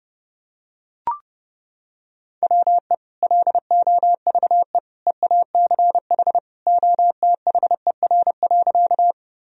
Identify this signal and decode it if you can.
Morse code, heard as a keyed tone, about 32 wpm, 700 Hz. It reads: WE LOVE EACH OTHER.
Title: Sentences from Top 1000 Words - Review Part 1 - 30wpm